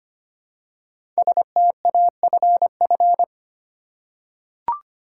Morse code, heard as a keyed tone, about 25 wpm, 700 Hz. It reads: STAFF
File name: 5 Letter Words - First 3 Letters Given - 25wpm